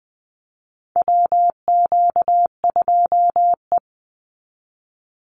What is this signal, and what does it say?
Morse code, heard as a keyed tone, about 19 wpm, 700 Hz. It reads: WQ2E
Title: Call Signs Repeated 2x - 20wpm